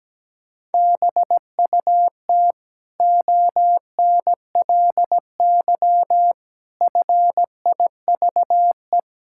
Morse code, heard as a keyed tone, about 17 wpm, 700 Hz. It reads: BUT ONLY FIVE
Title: Sentences from Top 1000 Words - Review Part 6 - 17wpm